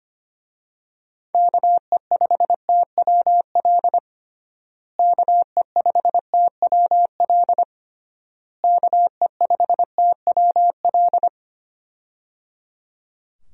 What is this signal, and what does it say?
Morse code, heard as a keyed tone, about 25 wpm, 700 Hz. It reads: KE5TWL KE5TWL KE5TWL
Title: USA Call Signs - 6 Characters - Repeated 3x - 25wpm